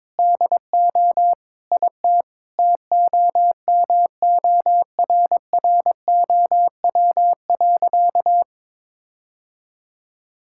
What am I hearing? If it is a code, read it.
Morse code, heard as a keyed tone, about 22 wpm, 700 Hz. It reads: DO IT TOMORROW.
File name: practice / 22wpm / Sentences with 4-Word Phrases - 22wpm